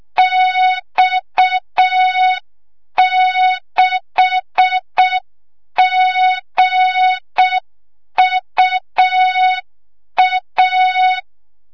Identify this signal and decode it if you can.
Morse code, heard as a keyed tone, about 6 wpm, 3700 Hz. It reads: X6GUA
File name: Morze-HA2MN / hang / 25 GYAKORLAT